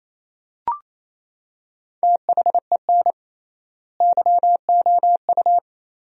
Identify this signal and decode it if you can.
Morse code, heard as a keyed tone, about 28 wpm, 700 Hz. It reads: THEN YOU
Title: 3x Word-Spacing - Sentences from Top 400 Words - 28wpm